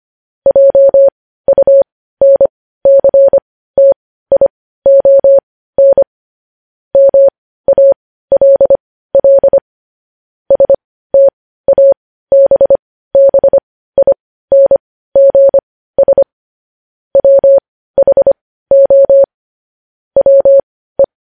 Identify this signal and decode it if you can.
Morse code, heard as a keyed tone, about 25 wpm, spaced as for 15 wpm, 550 Hz. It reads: JUNCTION MALL STABBINGS WHO WE
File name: podcast / News headlines-1x-25x15